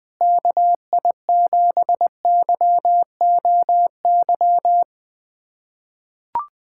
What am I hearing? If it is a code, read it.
Morse code, heard as a keyed tone, about 20 wpm, 700 Hz. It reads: KI7YOY
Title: USA Callsign DE Callsign - 6 Characters - 20wpm